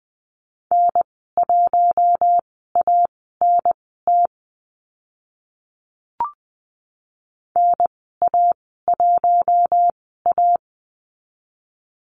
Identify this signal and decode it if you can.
Morse code, heard as a keyed tone, about 20 wpm, spaced as for 15 wpm, 700 Hz. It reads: N1ANT NA1A
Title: 048 Lesson 9g - Number 4 - Call Signs 20wpm